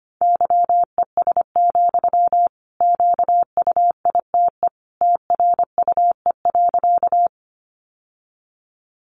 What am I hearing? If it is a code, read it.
Morse code, heard as a keyed tone, about 25 wpm, 700 Hz. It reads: YES, QUITE TRUE.
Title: Sentences from Top 700 Words 25wpm